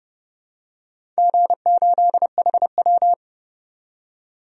Morse code, heard as a keyed tone, about 32 wpm, 700 Hz. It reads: G8HW
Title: DXCC-100 Call Signs - 4 Characters - 30wpm